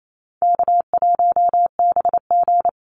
Morse code, heard as a keyed tone, about 28 wpm, 700 Hz. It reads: K1BG
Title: Call Signs - 28wpm